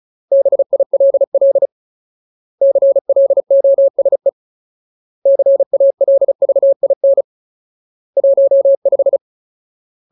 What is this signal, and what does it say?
Morse code, heard as a keyed tone, about 34 wpm, 550 Hz. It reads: BILL CLOSE CALVIN 15
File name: Ultimate Word List / 35WPM Ultimate